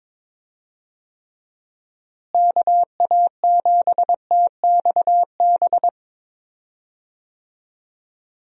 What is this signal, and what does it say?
Morse code, heard as a keyed tone, about 21 wpm, 700 Hz. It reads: KA7TXB
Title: USA Call Signs - 6 Characters - Repeated 2x - 22wpm